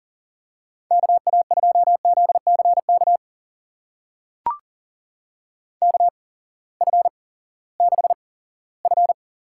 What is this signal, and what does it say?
Morse code, heard as a keyed tone, about 38 wpm, spaced as for 20 wpm, 700 Hz. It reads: KA2ZCK KF6F
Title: USA Call Signs - 6 Characters - 8x Character Spacing - 40wpm